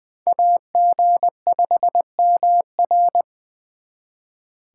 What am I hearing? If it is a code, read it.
Morse code, heard as a keyed tone, about 23 wpm, 700 Hz. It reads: AG5MR
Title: Speed Racing - USA 2x2 Call Signs 20wpm